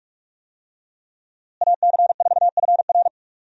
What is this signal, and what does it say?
Morse code, heard as a keyed tone, about 46 wpm, 700 Hz. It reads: AC4FR
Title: USA Call Signs - 5 Characters - Repeated 2x - 45wpm